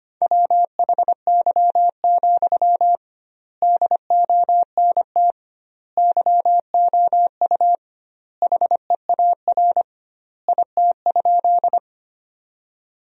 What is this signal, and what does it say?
Morse code, heard as a keyed tone, about 25 wpm, 700 Hz. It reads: WHY, DONT YOU HEAR IT?
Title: 2x Word-Spacing - Sentences from Top 300 Words - 25wpm